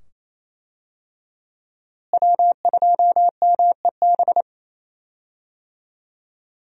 Morse code, heard as a keyed tone, about 28 wpm, 700 Hz. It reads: W2MEB
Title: USA Call Signs - 5 Characters - Repeated 2x - 28wpm